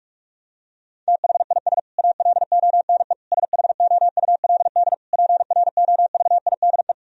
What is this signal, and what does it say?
Morse code, heard as a keyed tone, about 46 wpm, 700 Hz. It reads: THIS ALONE SHOULD PROVIDE
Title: Sentences from Top 1000 Words - Review Part 4 - 45wpm